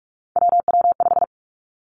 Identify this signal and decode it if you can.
Morse code, heard as a keyed tone, about 46 wpm, 700 Hz. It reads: WW5
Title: Three Challenging Letter-Number Combinations 45wpm